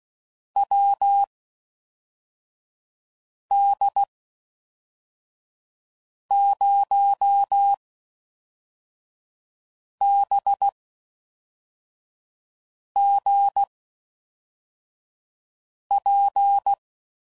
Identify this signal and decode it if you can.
Morse code, heard as a keyed tone, about 16 wpm, spaced as for 4 wpm, 800 Hz. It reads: WD0BGP